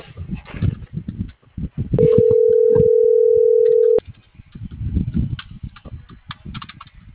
Ringback tone (North America), 440 + 480 Hz.